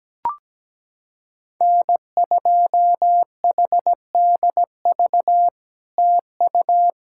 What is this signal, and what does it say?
Morse code, heard as a keyed tone, about 17 wpm, 700 Hz. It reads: N2HDV TU
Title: SOTA QSO Practice 17wpm